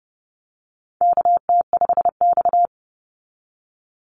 Morse code, heard as a keyed tone, about 32 wpm, 700 Hz. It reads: KT5X
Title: NA SOTA Chaser Callsigns 30wpm